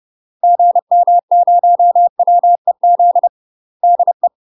Morse code, heard as a keyed tone, about 30 wpm, 700 Hz. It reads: GM0WEZ DE
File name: UK MASTER QSOs TRAINING / 30WPM / UK MASTER QSOs 0005